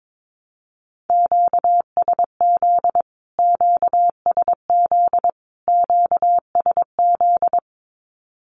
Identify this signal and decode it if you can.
Morse code, heard as a keyed tone, about 22 wpm, 700 Hz. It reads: QSZ QSZ QSZ
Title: Q-Codes Repeated 3x 22wpm